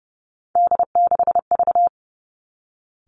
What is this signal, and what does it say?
Morse code, heard as a keyed tone, about 30 wpm, 700 Hz. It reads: D6V
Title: Three Challenging Letter-Number Combinations 30wpm